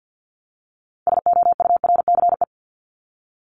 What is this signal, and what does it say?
Morse code, heard as a keyed tone, about 50 wpm, 700 Hz. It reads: SOURCE
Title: Top 501-750 Words 50wpm